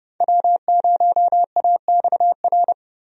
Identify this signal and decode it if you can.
Morse code, heard as a keyed tone, about 30 wpm, 700 Hz. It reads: W0AXR
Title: USA Call Signs - 5 Characters - 30wpm